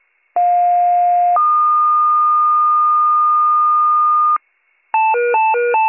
Two-tone page, heard as a one-second tone, then a three-second tone, about 710 then 1220 Hz. It is heard from 0.4 to 4.4 s.